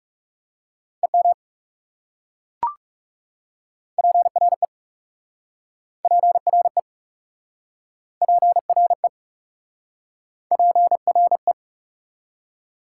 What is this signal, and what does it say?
Morse code, heard as a keyed tone, about 40 wpm, 690 Hz. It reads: EM PRE PRE PRE PRE